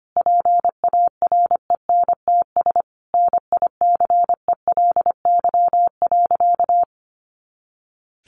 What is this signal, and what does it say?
Morse code, heard as a keyed tone, about 25 wpm, 700 Hz. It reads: PARENTS NICELY.
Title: Easy Phrases 25wpm